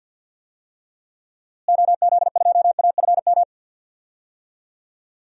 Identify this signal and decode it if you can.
Morse code, heard as a keyed tone, about 47 wpm, 700 Hz. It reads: OZ2AFM
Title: DXCC-100 Call Signs - 6 Characters - Repeated 2x - 50wpm